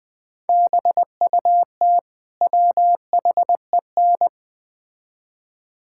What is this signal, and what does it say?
Morse code, heard as a keyed tone, about 20 wpm, 700 Hz. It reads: BUT WHEN
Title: Build A Phrase - 5-Word Phrases - 20wpm